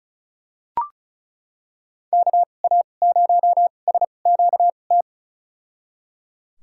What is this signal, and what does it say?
Morse code, heard as a keyed tone, about 34 wpm, spaced as for 25 wpm, 700 Hz. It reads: KA0SQT
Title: USA Call Signs - 6 Characters - 2x Character Spacing - 35wpm